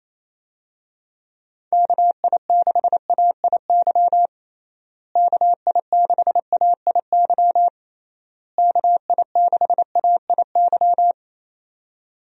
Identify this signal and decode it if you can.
Morse code, heard as a keyed tone, about 27 wpm, 700 Hz. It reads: KI6AIY KI6AIY KI6AIY
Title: USA Call Signs - 6 Characters - Repeated 3x - 28wpm